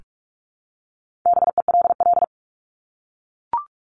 Morse code, heard as a keyed tone, about 44 wpm, 700 Hz. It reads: BELL
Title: Top 1751-2000 Words 45wpm